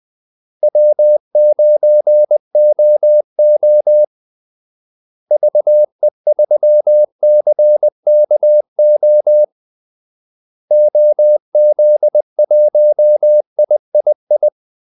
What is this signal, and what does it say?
Morse code, heard as a keyed tone, about 20 wpm, 600 Hz. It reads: W9OO VE3CKO OZ1III
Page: Kaldesignaler 20 wpm